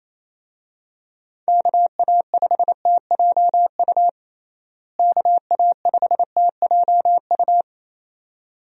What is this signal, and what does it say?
Morse code, heard as a keyed tone, about 28 wpm, 700 Hz. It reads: KA5TJU KA5TJU
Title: USA Call Signs - 6 Characters - Repeated 2x - 28wpm